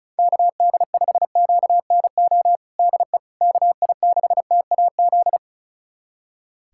Morse code, heard as a keyed tone, about 36 wpm, 700 Hz. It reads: KD5QNO DE KI6TAZ
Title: USA Callsign DE Callsign 35wpm